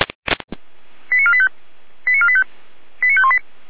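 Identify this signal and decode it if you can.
Five-tone selective call (ZVEI), heard as quick runs of five tones, tones of about 70 ms. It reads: 89485 89485 89518
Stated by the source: ZVEI five-tone selective call received via SDR.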